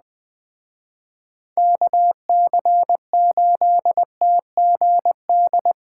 Morse code, heard as a keyed tone, about 20 wpm, 700 Hz. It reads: KC8TGD